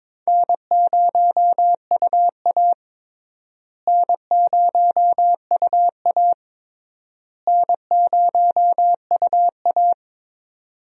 Morse code, heard as a keyed tone, about 22 wpm, 700 Hz. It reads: N0UA N0UA N0UA
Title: USA Call Signs - 4 Characters - Repeated 3x - 22wpm